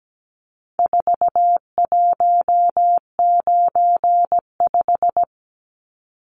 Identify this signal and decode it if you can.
Morse code, heard as a keyed tone, about 17 wpm, 700 Hz. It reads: 4195
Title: Numbers - 4-Digit 17wpm